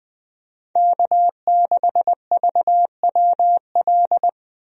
Morse code, heard as a keyed tone, about 20 wpm, 700 Hz. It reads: K6VWL
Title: USA Call Signs - 5 Characters - Repeated 2x - 20wpm